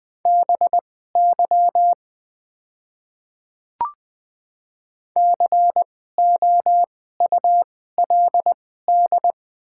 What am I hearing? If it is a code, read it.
Morse code, heard as a keyed tone, about 20 wpm, spaced as for 15 wpm, 700 Hz. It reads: BY COULD
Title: 2x Character Spacing - Top 200 Words - 20wpm